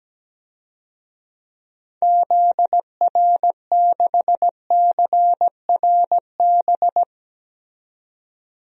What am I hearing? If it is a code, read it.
Morse code, heard as a keyed tone, about 17 wpm, 700 Hz. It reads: ZR6CRB
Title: DXCC-100 Call Signs - 6 Characters - 17wpm